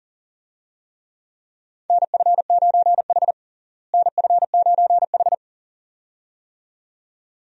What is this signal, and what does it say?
Morse code, heard as a keyed tone, about 40 wpm, 700 Hz. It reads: NF9H NF9H
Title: USA Call Signs - 4 Characters - Repeated 2x - 40wpm